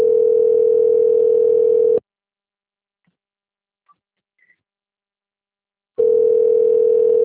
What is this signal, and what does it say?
Ringback tone (North America), 440 + 480 Hz.
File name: rg-1600-unknown- - - .22309